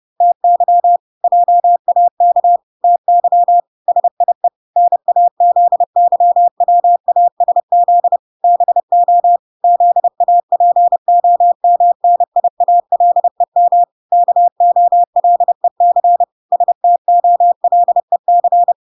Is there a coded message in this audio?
Yes: Morse code from 0.2 to 18.7 s, heard as a keyed tone, about 30 wpm, 700 Hz. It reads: TY JAK TY SIE NAZYWASZ BO ZAPOMNIALEM KOLEC STOLEC